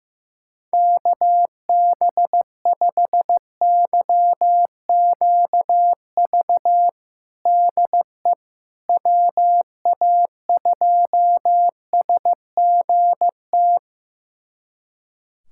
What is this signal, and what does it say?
Morse code, heard as a keyed tone, about 15 wpm, 700 Hz. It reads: KB5YQV DE WA2SGT
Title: USA Callsign DE Callsign 15wpm